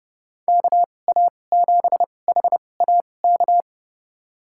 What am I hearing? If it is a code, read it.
Morse code, heard as a keyed tone, about 30 wpm, spaced as for 22 wpm, 700 Hz. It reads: KA7HAK
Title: USA Call Signs - 2x Character Spacing - 30wpm